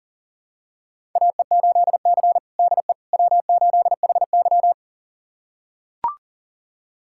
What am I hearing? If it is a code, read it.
Morse code, heard as a keyed tone, about 40 wpm, 700 Hz. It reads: AE8C DE W8HY
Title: USA Callsign DE Callsign - 4 Characters - 40wpm